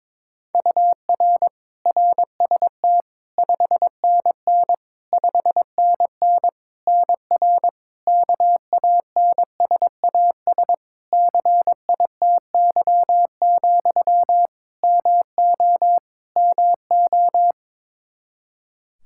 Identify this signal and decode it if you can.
Morse code, heard as a keyed tone, about 22 wpm, 700 Hz. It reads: UR RST 5NN 5NN NR KANSAS CITY, MO MO
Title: Practice QSO - Exchange 1 - Beginning - 22wpm